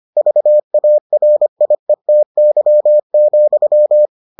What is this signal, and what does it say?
Morse code, heard as a keyed tone, about 25 wpm, 600 Hz. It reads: VARIETY,